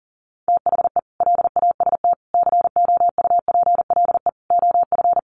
Morse code, heard as a keyed tone, about 40 wpm, 700 Hz. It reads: THE LAST COUPLE OF
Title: Sets of 5 Words Encore 40wpm